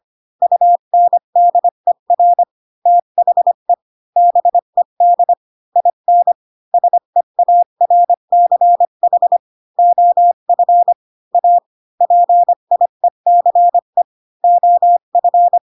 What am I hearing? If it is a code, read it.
Morse code, heard as a keyed tone, about 25 wpm, 700 Hz. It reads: UNDER THE BED IN SEARCH OF A PIECE OF
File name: Pinocchio / Pinocchio ch5 20 25 ik1hge